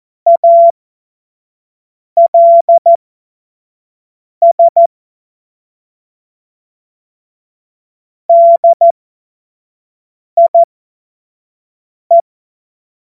Morse code, heard as a keyed tone, about 14 wpm, spaced as for 5 wpm, 680 Hz. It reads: ALS DIE
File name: Morsekurs / Witze / Witz 25. April 2023 25 F70